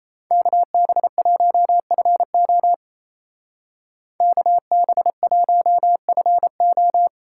Morse code, heard as a keyed tone, about 33 wpm, 700 Hz. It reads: KB1FO KB1FO
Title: Speed Racing - USA 2x2 Call Signs 28wpm